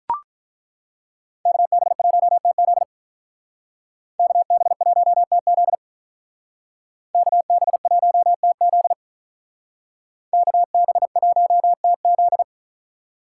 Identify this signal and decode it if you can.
Morse code, heard as a keyed tone, about 42 wpm, 700 Hz. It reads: KB1TZ KB1TZ KB1TZ KB1TZ
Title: Speed Racing - USA 2x2 Call Signs 35wpm